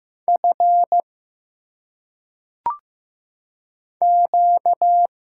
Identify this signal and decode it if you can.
Morse code, heard as a keyed tone, about 15 wpm, 700 Hz. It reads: F Q
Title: Single Letters - Phonetic Pronunciation 15wpm